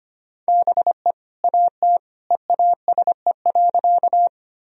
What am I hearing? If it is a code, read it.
Morse code, heard as a keyed tone, about 25 wpm, 700 Hz. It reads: BE AT EASE.
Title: Sentences from Top 1000 Words - Review Part 1 - 25wpm